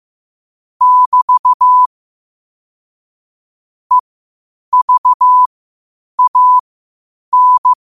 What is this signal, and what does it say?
Morse code, heard as a keyed tone, about 15 wpm, spaced as for 8 wpm, 1000 Hz. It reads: <BT> EVAN
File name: qotdfiles / 8wpm